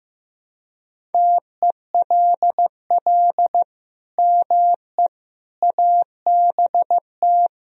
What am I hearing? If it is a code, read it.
Morse code, heard as a keyed tone, about 16 wpm, 700 Hz. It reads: TELL ME ABT
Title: Sentences from Top 200 Words and Abbreviations 15wpm